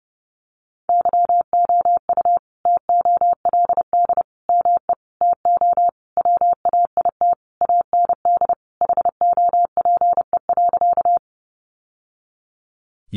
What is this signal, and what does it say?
Morse code, heard as a keyed tone, about 30 wpm, 700 Hz. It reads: YOU TOLD ME TO WAIT AND HOPE.
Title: Sentences from Top 1000 Words 30wpm